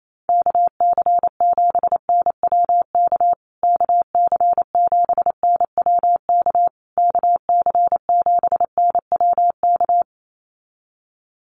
Morse code, heard as a keyed tone, about 28 wpm, 700 Hz. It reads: KC7NWK KC7NWK KC7NWK
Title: USA Call Signs Repeated 3x 28wpm